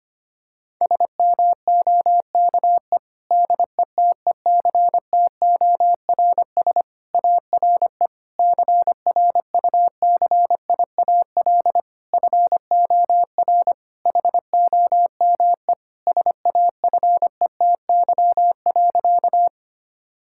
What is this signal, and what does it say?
Morse code, heard as a keyed tone, about 25 wpm, 700 Hz. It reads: SMOKE DETECTORS ARE CRUCIAL FOR HOME SAFETY.
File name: practice / 25wpm / Difficult Sentences with Hints - 25wpm